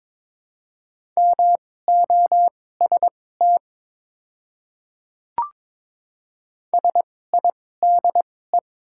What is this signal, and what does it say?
Morse code, heard as a keyed tone, about 21 wpm, spaced as for 16 wpm, 700 Hz. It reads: MOST SIDE
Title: 2x Character Spacing - Top 300 Words - 22wpm